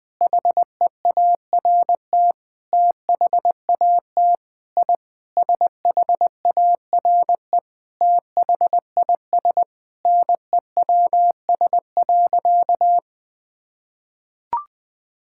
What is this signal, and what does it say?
Morse code, heard as a keyed tone, about 21 wpm, 700 Hz. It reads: HEART THAT I SHARE THIS NEWS.
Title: Sentences with 8-Word Phrases - Spoken Beginning - 20wpm